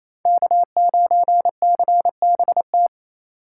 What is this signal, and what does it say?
Morse code, heard as a keyed tone, about 33 wpm, 700 Hz. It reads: K9CBT
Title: Speed Racing - USA 1x3 Call Signs 28wpm